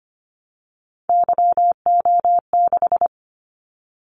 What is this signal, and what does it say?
Morse code, heard as a keyed tone, about 25 wpm, 700 Hz. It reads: YO6
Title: Callsign Prefixes for Top 25 DXCC - 25wpm